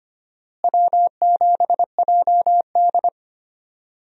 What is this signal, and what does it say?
Morse code, heard as a keyed tone, about 25 wpm, 700 Hz. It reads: W7JD
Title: USA Call Signs - 4 Characters - Repeated 3x - 25wpm